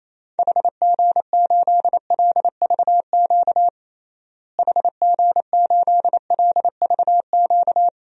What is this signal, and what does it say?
Morse code, heard as a keyed tone, about 28 wpm, 700 Hz. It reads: HG8LVQ HG8LVQ
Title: DXCC-100 Call Signs - 6 Characters - Repeated 3x - 28wpm